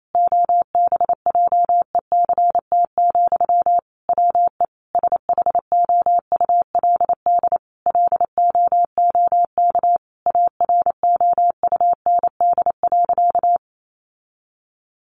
Morse code, heard as a keyed tone, about 28 wpm, 700 Hz. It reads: OBJECT, WE SHOULD LOOK AROUND.
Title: Sentences from Top 800 Words 28wpm